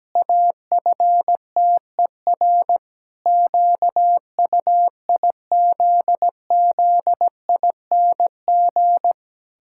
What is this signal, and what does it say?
Morse code, heard as a keyed tone, about 17 wpm, 700 Hz. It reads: AFTER QUIZZING
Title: Pangram Challenge v2 - 17wpm